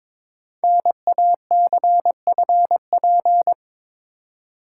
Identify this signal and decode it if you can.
Morse code, heard as a keyed tone, about 23 wpm, 700 Hz. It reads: NACFP
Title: Code Groups - 5 Letter:Numbers - 22wpm